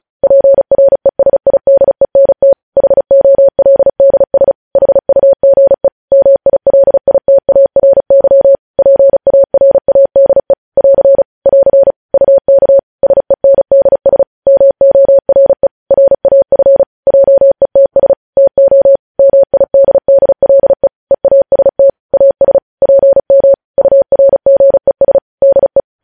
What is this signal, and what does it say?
Morse code, heard as a keyed tone, about 36 wpm, 550 Hz. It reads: PRESIDENT HOLDS HUGE MILITARY PARADE <AR> <AR> UK SENDS MORE RAF JETS TO MIDDLE EAST AS PM URGES DE
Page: news headlines in morse code at 35 wpm